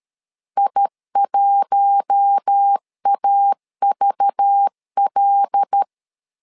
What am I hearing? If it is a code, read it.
Morse code, heard as a keyed tone, about 13 wpm, 800 Hz. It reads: I1AVL
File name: LEZIONI / 5 LBVC / 5 LBV 12wpm 1855